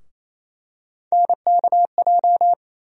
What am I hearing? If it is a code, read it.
Morse code, heard as a keyed tone, about 28 wpm, 700 Hz. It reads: NKJ
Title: USA Call Sign Postfixes - 3 Letters 28wpm